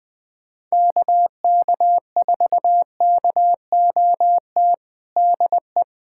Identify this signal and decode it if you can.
Morse code, heard as a keyed tone, about 20 wpm, 700 Hz. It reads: KK4KOT DE
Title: USA Callsign DE Callsign 20wpm